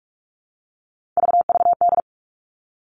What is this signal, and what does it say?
Morse code, heard as a keyed tone, about 46 wpm, 700 Hz. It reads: VVD
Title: Three Challenging Letter-Number Combinations 45wpm